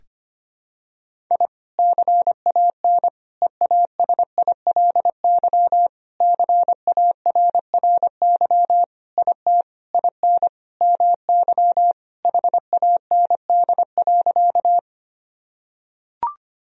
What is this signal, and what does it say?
Morse code, heard as a keyed tone, about 25 wpm, 700 Hz. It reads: I CAN EASILY CARRY IT IN MY HAND.
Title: Sentences from Top 1000 Words - Review Part 1 - 25wpm